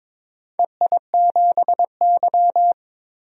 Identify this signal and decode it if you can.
Morse code, heard as a keyed tone, about 22 wpm, 700 Hz. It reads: EI7Y
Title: DXCC-100 Call Signs - 4 Characters - 22wpm